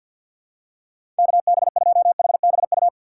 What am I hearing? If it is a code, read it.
Morse code, heard as a keyed tone, about 53 wpm, 700 Hz. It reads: KB2HBU
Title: USA Call Signs - 6 Characters - 6x Character Spacing - 50wpm